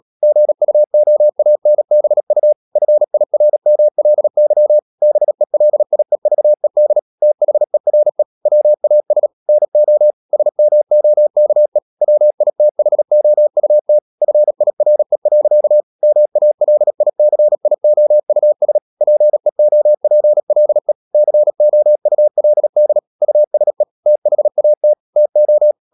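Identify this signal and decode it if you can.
Morse code, heard as a keyed tone, about 37 wpm, 600 Hz. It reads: GUOANBU FIRMLY BELIEVED THERE WAS NO SMOKE WITHOUT FIRE. MALICIOUS PEOPLE COULD USE THAT TO
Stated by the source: Morse code practice at 35–39 wpm sending English text of Ken Follett's novel Never, from CW med Gnister.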